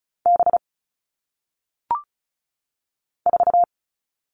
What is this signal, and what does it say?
Morse code, heard as a keyed tone, about 36 wpm, 700 Hz. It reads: B 4
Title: All Cut Numbers - 1-Digit 35wpm